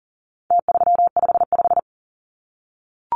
Morse code, heard as a keyed tone, about 40 wpm, 700 Hz. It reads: T355
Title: Numbers - Contest Style - 4-Digit 40wpm